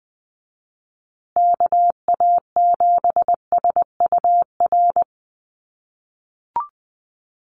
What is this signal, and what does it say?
Morse code, heard as a keyed tone, about 20 wpm, 700 Hz. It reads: KA7SUR
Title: USA Call Signs Repeated 2x 20wpm